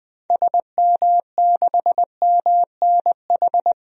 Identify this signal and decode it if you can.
Morse code, heard as a keyed tone, about 20 wpm, 700 Hz. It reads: SM6MNH
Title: DXCC-100 Call Signs - 6 Characters - Repeated 2x - 20wpm